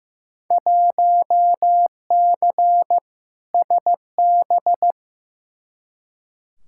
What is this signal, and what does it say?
Morse code, heard as a keyed tone, about 15 wpm, 700 Hz. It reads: 1C SB
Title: Field Day Exchanges Encore - 15wpm